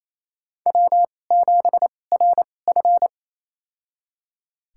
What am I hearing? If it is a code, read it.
Morse code, heard as a keyed tone, about 28 wpm, spaced as for 20 wpm, 700 Hz. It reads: W7RF
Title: USA Call Signs - 4 Characters - 2x Character Spacing - 28wpm